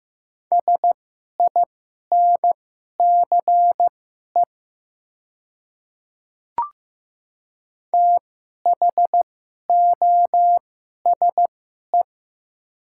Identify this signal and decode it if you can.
Morse code, heard as a keyed tone, about 15 wpm, spaced as for 11 wpm, 700 Hz. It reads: SINCE THOSE
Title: 2x Character Spacing - Top 300 Words - 15wpm